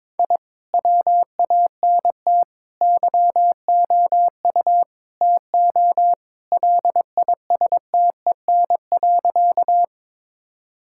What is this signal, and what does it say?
Morse code, heard as a keyed tone, about 22 wpm, 700 Hz. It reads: I WANT YOU TO LISTEN.
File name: practice / 22wpm / Sentences with 4-Word Phrases - 22wpm